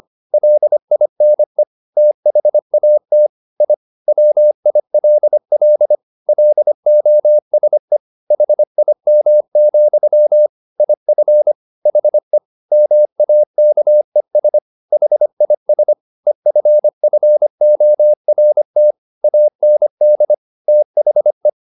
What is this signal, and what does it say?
Morse code, heard as a keyed tone, about 25 wpm, 600 Hz. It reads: LINE THAT I WILL LOSE HIM, IF HE MAKES HIS EFFORT AND THE